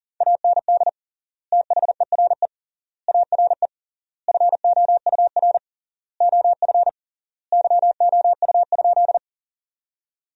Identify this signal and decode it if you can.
Morse code, heard as a keyed tone, about 40 wpm, 700 Hz. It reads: AND THERE ARE FOUR OF YOU?
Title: 3x Word-Spacing - Sentences from Top 300 Words - 40wpm